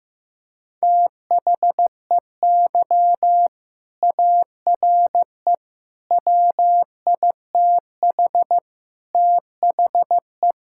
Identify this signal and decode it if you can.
Morse code, heard as a keyed tone, about 15 wpm, 700 Hz. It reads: THEY ARE WITH THE
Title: Sentences from Top 1000 Words - Review Part 5 - 15wpm